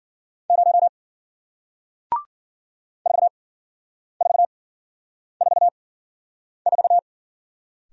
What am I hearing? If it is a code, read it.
Morse code, heard as a keyed tone, about 53 wpm, spaced as for 9 wpm, 700 Hz. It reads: 0 4444